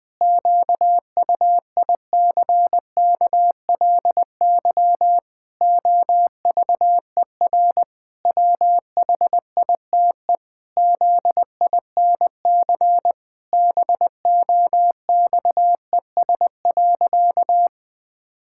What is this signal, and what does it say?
Morse code, heard as a keyed tone, about 21 wpm, 700 Hz. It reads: QUICKLY OVER WHITE ZINC BOXES.
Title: Pangram Challenge v2 - 20wpm